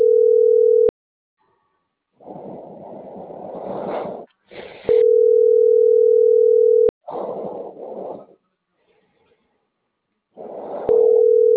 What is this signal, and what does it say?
Ringback tone (North America), 440 + 480 Hz.